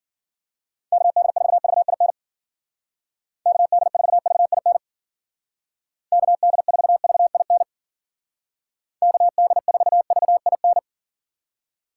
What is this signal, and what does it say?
Morse code, heard as a keyed tone, about 49 wpm, 700 Hz. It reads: KD4VIN KD4VIN KD4VIN KD4VIN
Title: Speed Racing - USA 2x3 Call Signs 40wpm